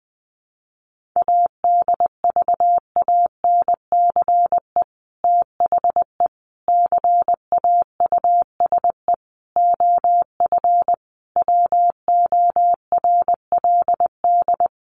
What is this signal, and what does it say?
Morse code, heard as a keyed tone, about 20 wpm, 700 Hz. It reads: ADVANCE THE CAUSE OF WORLD
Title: Sentences from Top 1400 Words 20wpm